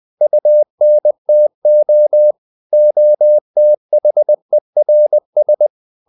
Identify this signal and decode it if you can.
Morse code, heard as a keyed tone, about 20 wpm, 600 Hz. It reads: UNTO OTHERS
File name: CW / Practice / EnglishSayings20